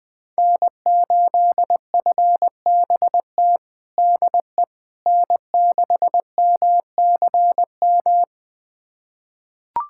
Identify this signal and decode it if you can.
Morse code, heard as a keyed tone, about 20 wpm, 700 Hz. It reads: N8FBT DE N6MCM
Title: USA Callsign DE Callsign - 5 Characters - 20wpm